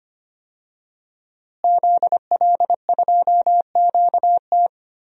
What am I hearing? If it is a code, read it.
Morse code, heard as a keyed tone, about 25 wpm, 700 Hz. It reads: ZL2QT
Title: DXCC-100 Call Signs - 5 Characters - Repeated 3x - 25wpm